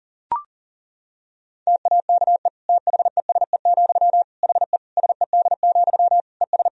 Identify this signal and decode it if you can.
Morse code, heard as a keyed tone, about 40 wpm, 700 Hz. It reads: TAKE THESE, HE SED, ES
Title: Sentences from Top 300 Words and Abbreviations 40wpm